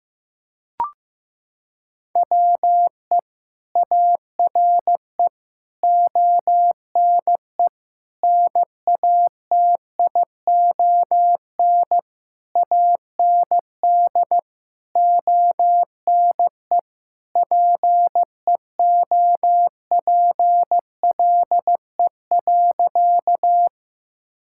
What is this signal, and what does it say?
Morse code, heard as a keyed tone, about 15 wpm, 700 Hz. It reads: WE ARE ONE NATION AND ONE PEOPLE.
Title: Sentences from Top 1000 Words - Review Part 6 - 15wpm